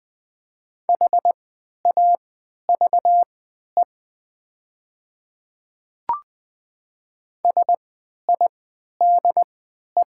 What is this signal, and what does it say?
Morse code, heard as a keyed tone, about 20 wpm, spaced as for 11 wpm, 700 Hz. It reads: HAVE SIDE
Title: 3x Character Spacing - Top 100 Words - 20wpm